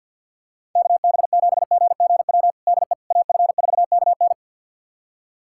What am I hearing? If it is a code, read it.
Morse code, heard as a keyed tone, about 52 wpm, 700 Hz. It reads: KB7GGW DE AF4KN
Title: USA Callsign DE Callsign 50wpm